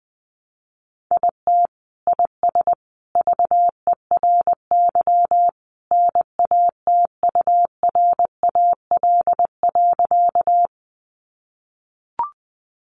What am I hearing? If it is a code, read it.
Morse code, heard as a keyed tone, about 19 wpm, 700 Hz. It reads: IT IS VERY NATURAL.
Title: Sentences from Top 700 Words 20wpm